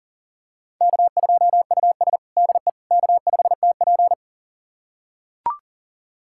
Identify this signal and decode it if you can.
Morse code, heard as a keyed tone, about 40 wpm, 700 Hz. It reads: K2US DE K5TP
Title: USA Callsign DE Callsign - 4 Characters - 40wpm